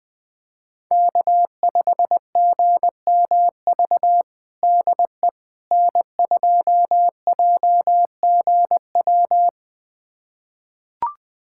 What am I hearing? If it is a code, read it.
Morse code, heard as a keyed tone, about 21 wpm, 700 Hz. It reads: K5GMV DE N2JGW
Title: USA Callsign DE Callsign - 5 Characters - 20wpm